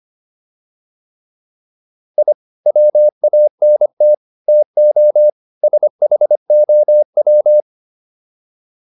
Morse code, heard as a keyed tone, about 25 wpm, 600 Hz. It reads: I WANT TO SHOW